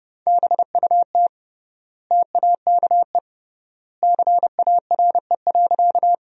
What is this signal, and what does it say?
Morse code, heard as a keyed tone, about 30 wpm, 700 Hz. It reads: BUT TAKE CARE.
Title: 3x Word-Spacing - Sentences from Top 300 Words - 30wpm